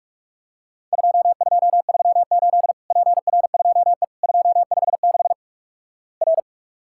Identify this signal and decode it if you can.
Morse code, heard as a keyed tone, about 48 wpm, 700 Hz. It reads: 2238 PR2E 256 R
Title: Worked All Europe DX Contest - QTC Exchange - 45wpm